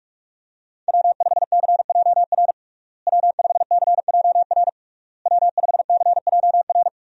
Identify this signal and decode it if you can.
Morse code, heard as a keyed tone, about 47 wpm, 700 Hz. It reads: W5CJR W5CJR W5CJR
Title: USA Call Signs - 5 Characters - Repeated 3x - 45wpm